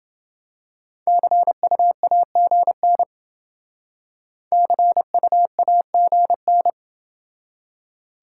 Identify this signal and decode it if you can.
Morse code, heard as a keyed tone, about 28 wpm, 700 Hz. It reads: CUAGN CUAGN